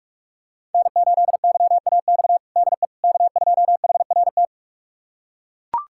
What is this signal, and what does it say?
Morse code, heard as a keyed tone, about 42 wpm, 700 Hz. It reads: N8YAX DE K2HRT
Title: USA Callsign DE Callsign - 5 Characters - 45wpm